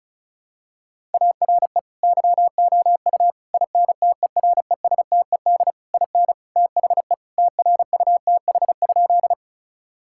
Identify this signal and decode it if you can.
Morse code, heard as a keyed tone, about 34 wpm, 700 Hz. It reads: ARE YOU INTERESTED IN THE TRUTH?
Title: Sentences from Top 1000 Words - Review Part 6 - 35wpm